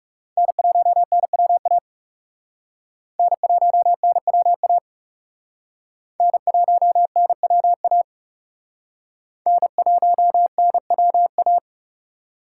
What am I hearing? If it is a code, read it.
Morse code, heard as a keyed tone, about 37 wpm, 700 Hz. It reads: N1NWA N1NWA N1NWA N1NWA
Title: Speed Racing - USA Call Signs 30wpm